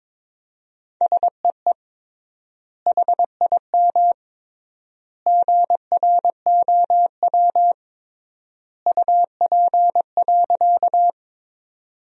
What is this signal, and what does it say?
Morse code, heard as a keyed tone, about 22 wpm, 700 Hz. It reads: SEE HIM GROW UP.
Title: 3x Word-Spacing - Sentences from Top 300 Words - 22wpm